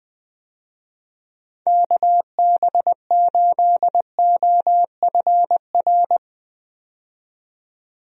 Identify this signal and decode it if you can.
Morse code, heard as a keyed tone, about 20 wpm, 700 Hz. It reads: KB8OFR
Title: USA Call Signs - 6 Characters - 20wpm